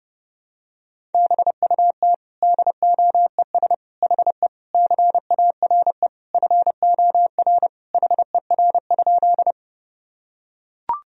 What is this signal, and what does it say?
Morse code, heard as a keyed tone, about 32 wpm, 700 Hz. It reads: BUT DOES HE CARE FOR HER?
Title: Sentences from Top 1000 Words - Review Part 1 - 30wpm